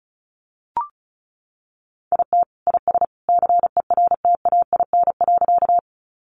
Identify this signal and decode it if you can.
Morse code, heard as a keyed tone, about 34 wpm, 700 Hz. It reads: IT IS CERTAIN.
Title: Sentences from Top 500 Words 35wpm